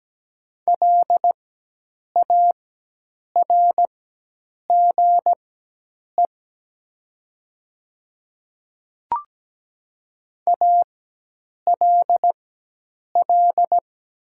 Morse code, heard as a keyed tone, about 17 wpm, spaced as for 8 wpm, 700 Hz. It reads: LARGE ALL
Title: 4x Character Spacing - Top 200 Words - 17wpm